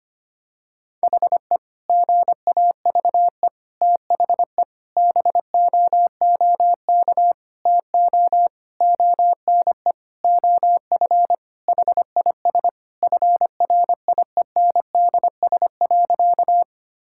Morse code, heard as a keyed tone, about 25 wpm, 700 Hz. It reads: HE GAVE THE BOOK TO ONE OF HIS FRIENDS.
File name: practice / 25wpm / Sentences with 4-Word Phrases - 25wpm